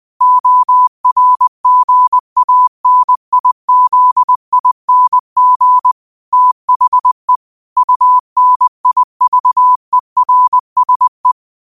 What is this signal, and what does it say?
Morse code, heard as a keyed tone, about 20 wpm, 1000 Hz. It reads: ORGANIZING THE UNIVERSE
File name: qotdfiles / 20wpm